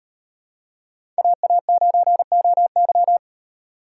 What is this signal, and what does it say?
Morse code, heard as a keyed tone, about 38 wpm, 700 Hz. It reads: AA9OY